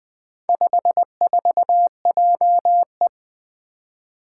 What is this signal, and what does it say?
Morse code, heard as a keyed tone, about 20 wpm, 700 Hz. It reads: 54JE
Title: Four Challenging Letter-Number Combinations 20wpm